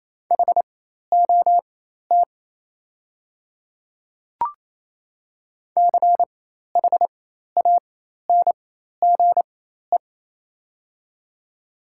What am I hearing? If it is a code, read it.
Morse code, heard as a keyed tone, about 27 wpm, spaced as for 13 wpm, 690 Hz. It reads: HOT CHANGE
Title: 4x Character Spacing - Top 300 Words - 28wpm